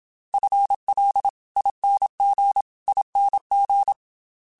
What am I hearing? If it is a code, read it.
Morse code, heard as a keyed tone, about 27 wpm, 800 Hz. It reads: FLINGING
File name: FL / FL mp3s / FL Prefixes 3x2 sp @25